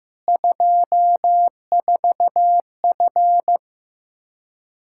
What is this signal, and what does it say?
Morse code, heard as a keyed tone, about 16 wpm, 700 Hz. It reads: 24F
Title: Practice QSO - Exchange 2 - Info - 15wpm